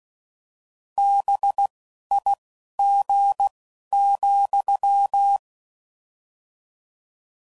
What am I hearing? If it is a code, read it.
Morse code, heard as a keyed tone, about 16 wpm, spaced as for 11 wpm, 800 Hz. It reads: BIG,